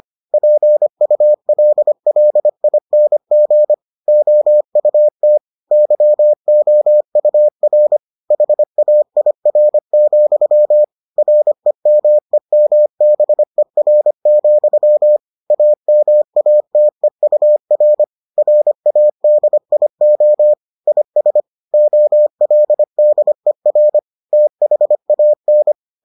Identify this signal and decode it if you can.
Morse code, heard as a keyed tone, about 25 wpm, 600 Hz. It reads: PULLING OUT YOUR HAIR, REMEMBER, AMATEUR RADIO IS OLDER THAN